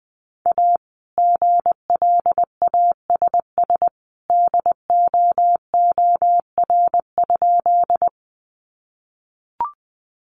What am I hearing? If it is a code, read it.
Morse code, heard as a keyed tone, about 20 wpm, 700 Hz. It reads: A GLASS DOOR?
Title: Sentences from Top 1300 Words 20wpm